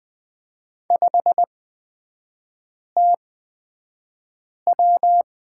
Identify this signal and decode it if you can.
Morse code, heard as a keyed tone, about 20 wpm, spaced as for 5 wpm, 700 Hz. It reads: 5TW
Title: 243 Lesson 39e - Letter z - Three Characters 20wpm